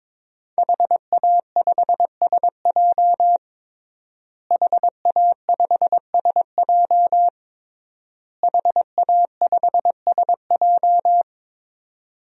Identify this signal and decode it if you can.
Morse code, heard as a keyed tone, about 22 wpm, 700 Hz. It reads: HA5SJ HA5SJ HA5SJ
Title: DXCC-100 Call Signs - 5 Characters - Repeated 3x - 22wpm